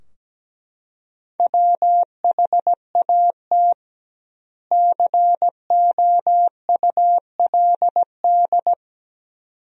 Morse code, heard as a keyed tone, about 17 wpm, 700 Hz. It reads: WHAT COULD
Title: 2x Word-Spacing - Sentences from Top 400 Words - 17wpm